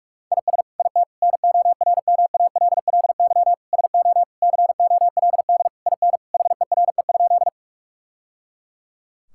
Morse code, heard as a keyed tone, about 46 wpm, 700 Hz. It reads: IS IT NORMALLY SO COLD IN HERE?
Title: Sentences from Top 1000 Words - Review Part 6 - 45wpm